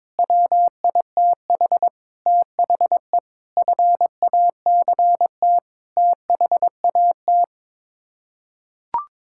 Morse code, heard as a keyed tone, about 22 wpm, 700 Hz. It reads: WITH THE FACT THAT
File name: practice / 22wpm / Sets of 8 Words - 22wpm